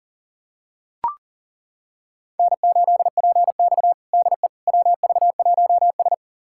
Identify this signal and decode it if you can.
Morse code, heard as a keyed tone, about 37 wpm, 700 Hz. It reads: N8PX DE WV1S
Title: USA Callsign DE Callsign - 4 Characters - 40wpm